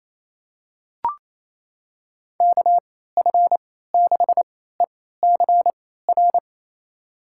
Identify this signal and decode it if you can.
Morse code, heard as a keyed tone, about 28 wpm, spaced as for 16 wpm, 700 Hz. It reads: KF6ECR
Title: USA Call Signs - 3x Character Spacing - 28wpm